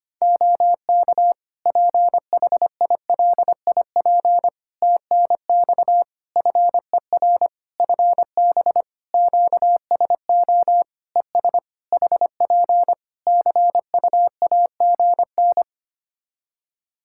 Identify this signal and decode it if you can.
Morse code, heard as a keyed tone, about 25 wpm, 700 Hz. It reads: OK PHILIP TNX FER FB QSO ES HP CUAGN
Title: Practice QSO - Exchange 4 - Ending - 25wpm